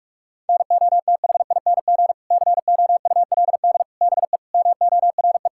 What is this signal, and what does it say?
Morse code, heard as a keyed tone, about 46 wpm, 700 Hz. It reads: NOTHING COULD BE MORE
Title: Sentences from Top 1600 Words 45wpm